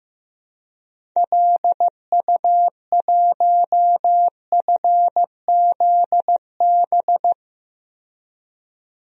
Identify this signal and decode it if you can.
Morse code, heard as a keyed tone, about 16 wpm, 700 Hz. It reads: LU1FZB
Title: DXCC-100 Call Signs - 6 Characters - 15wpm